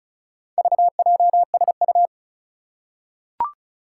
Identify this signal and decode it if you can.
Morse code, heard as a keyed tone, about 36 wpm, 700 Hz. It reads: VJSU
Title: Four Challenging Letter-Number Combinations 35wpm